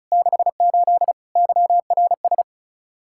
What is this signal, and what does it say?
Morse code, heard as a keyed tone, about 46 wpm, 700 Hz. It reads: 68 YRS